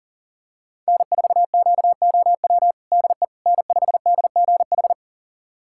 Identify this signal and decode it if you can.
Morse code, heard as a keyed tone, about 40 wpm, 700 Hz. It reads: N4QOW DE N5DGH
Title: USA Callsign DE Callsign - 5 Characters - 40wpm